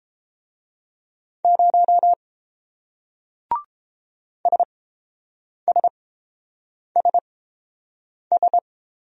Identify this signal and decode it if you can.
Morse code, heard as a keyed tone, about 30 wpm, spaced as for 7 wpm, 700 Hz. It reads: 0 SSSS